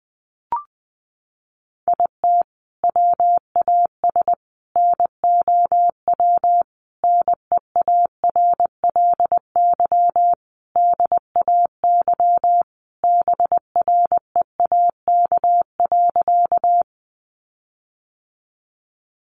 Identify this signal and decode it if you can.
Morse code, heard as a keyed tone, about 20 wpm, 700 Hz. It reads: IT WAS NOW NEARLY DAY BREAK.
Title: Sentences from Top 1300 Words 20wpm